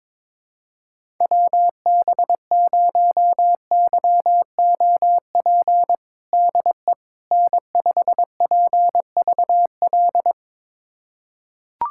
Morse code, heard as a keyed tone, about 22 wpm, 700 Hz. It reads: WB0YOP DE N5PVL
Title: USA Callsign DE Callsign 22wpm